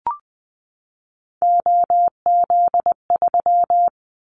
Morse code, heard as a keyed tone, about 20 wpm, 700 Hz. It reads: OZ3
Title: Callsign Prefixes for Top 100 DXCC - 20wpm